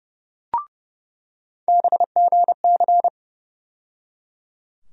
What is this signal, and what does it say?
Morse code, heard as a keyed tone, about 30 wpm, 700 Hz. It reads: BGC
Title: USA Call Sign Postfixes - 3 Letters 30wpm